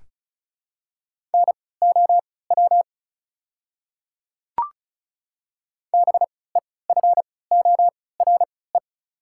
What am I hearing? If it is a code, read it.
Morse code, heard as a keyed tone, about 36 wpm, spaced as for 20 wpm, 690 Hz. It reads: NOW BEFORE
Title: 3x Character Spacing - Top 200 Words - 35wpm